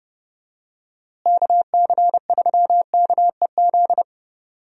Morse code, heard as a keyed tone, about 29 wpm, 700 Hz. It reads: KC3KEZ
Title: USA Call Signs - 6 Characters - Repeated 3x - 30wpm